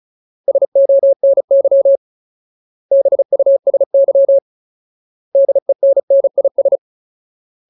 Morse code, heard as a keyed tone, about 34 wpm, 550 Hz. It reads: SONY BUSY DENNIS
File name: Ultimate Word List / 35WPM Ultimate